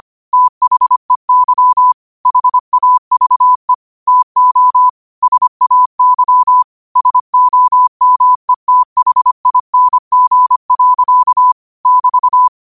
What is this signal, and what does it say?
Morse code, heard as a keyed tone, about 25 wpm, 1000 Hz. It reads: THEY HAVE TO SAY SOMETHING. <BT>
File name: qotdfiles / 25wpm